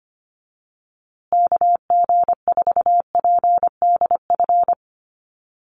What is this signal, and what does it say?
Morse code, heard as a keyed tone, about 25 wpm, 700 Hz. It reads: KG4PDF
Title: USA Call Signs Repeated 3x 25wpm